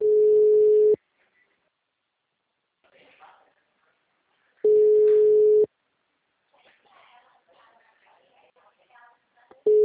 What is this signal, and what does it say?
Ringback tone (Europe, 425 Hz), with speech.
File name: out- -2005- - - .29648